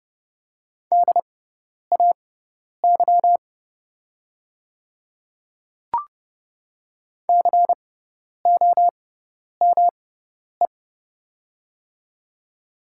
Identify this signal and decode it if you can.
Morse code, heard as a keyed tone, about 30 wpm, spaced as for 10 wpm, 700 Hz. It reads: DAY COME
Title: 6x Character Spacing - Top 100 Words - 30wpm